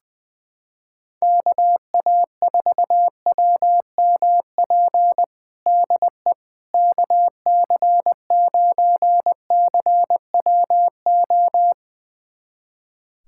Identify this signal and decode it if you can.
Morse code, heard as a keyed tone, about 21 wpm, 700 Hz. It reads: KA4WMP DE KC9CWO
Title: USA Callsign DE Callsign - 6 Characters - 20wpm